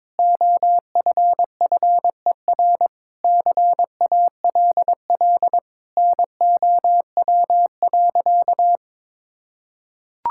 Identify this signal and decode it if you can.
Morse code, heard as a keyed tone, about 22 wpm, 700 Hz. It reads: OFFER CALL NOW.
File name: practice / 22wpm / Sentences with 4-Word Phrases - 22wpm